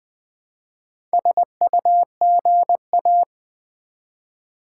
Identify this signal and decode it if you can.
Morse code, heard as a keyed tone, about 20 wpm, 700 Hz. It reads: SUGA
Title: Build A Word - 1 Letter At A Time - 3 to 5-Letter Words - 20wpm